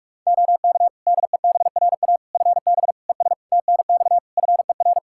Morse code, heard as a keyed tone, about 46 wpm, 700 Hz. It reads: OK DEBRA FB ES TNX FER
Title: Practice QSO - Exchange 2 - Info - 45wpm